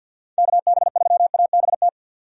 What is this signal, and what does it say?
Morse code, heard as a keyed tone, about 51 wpm, 700 Hz. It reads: KB3ABT
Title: USA Call Signs - 6 Characters - 50wpm